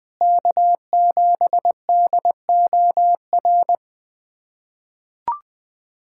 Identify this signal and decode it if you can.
Morse code, heard as a keyed tone, about 20 wpm, 700 Hz. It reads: K7DOR
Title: USA Callsign DE Callsign - 5 Characters - 20wpm